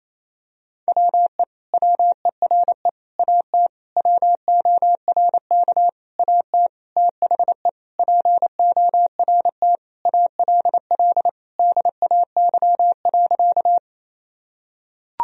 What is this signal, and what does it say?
Morse code, heard as a keyed tone, about 28 wpm, 700 Hz. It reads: WE WERE AT WORK AT THE PORT ALL DAY.
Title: Sentences from Top 1000 Words - Review Part 1 - 28wpm